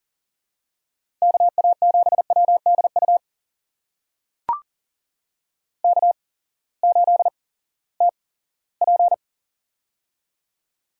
Morse code, heard as a keyed tone, about 40 wpm, spaced as for 20 wpm, 700 Hz. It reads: KA7WDU K8TP
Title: USA Call Signs - 8x Character Spacing - 40wpm